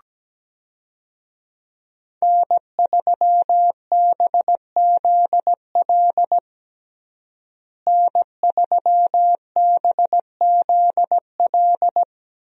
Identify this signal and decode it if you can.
Morse code, heard as a keyed tone, about 17 wpm, 700 Hz. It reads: N3BZL N3BZL
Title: USA Call Signs - 5 Characters - Repeated 3x - 17wpm